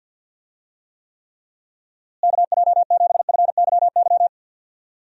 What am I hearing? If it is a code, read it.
Morse code, heard as a keyed tone, about 52 wpm, 700 Hz. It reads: KJ7FYY
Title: USA Call Signs - 6 Characters - Repeated 3x - 50wpm